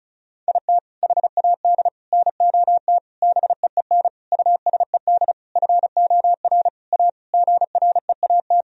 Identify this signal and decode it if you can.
Morse code, heard as a keyed tone, about 34 wpm, 700 Hz. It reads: IT HAD NOT BEEN USED FOR A GREAT
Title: Sentences from Top 1000 Words - Review Part 1 - 35wpm